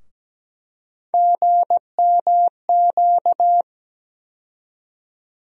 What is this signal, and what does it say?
Morse code, heard as a keyed tone, about 17 wpm, 700 Hz. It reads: GMQ
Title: USA Call Sign Postfixes - 3 Letters 17wpm